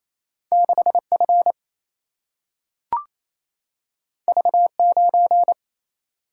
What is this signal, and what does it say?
Morse code, heard as a keyed tone, about 28 wpm, 700 Hz. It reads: 6 F V 9
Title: Single-Letter Single-Number 28wpm